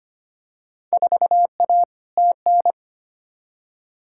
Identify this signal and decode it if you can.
Morse code, heard as a keyed tone, about 25 wpm, 700 Hz. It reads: 4A TN
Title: Field Day Exchanges Encore - 25wpm